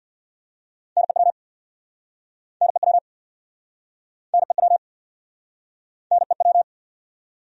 Morse code, heard as a keyed tone, about 68 wpm, 700 Hz. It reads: NEW NEW NEW NEW